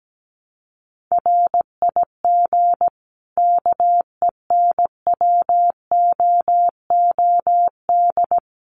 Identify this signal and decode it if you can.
Morse code, heard as a keyed tone, about 17 wpm, 700 Hz. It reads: RIG KENWOOD
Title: Common QSO Phrases 17wpm